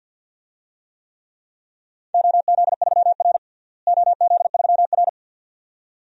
Morse code, heard as a keyed tone, about 51 wpm, 700 Hz. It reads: OZ3R OZ3R
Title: DXCC-100 Call Signs - 4 Characters - Repeated 2x - 50wpm